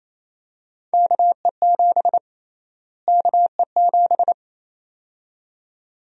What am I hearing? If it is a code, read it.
Morse code, heard as a keyed tone, about 28 wpm, 700 Hz. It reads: KE7 KE7
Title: USA Call Sign Prefixes - Repeated 2x - 28wpm